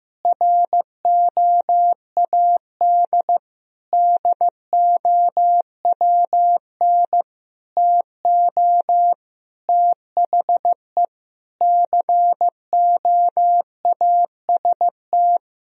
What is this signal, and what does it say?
Morse code, heard as a keyed tone, about 15 wpm, 700 Hz. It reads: ROAD DOWN TO THE COAST
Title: Sentences from Top 1000 Words - Review Part 4 - 15wpm